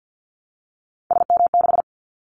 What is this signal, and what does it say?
Morse code, heard as a keyed tone, about 50 wpm, 700 Hz. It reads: SM6
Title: Callsign Prefixes for Top 25 DXCC - Repeated 3x - 50wpm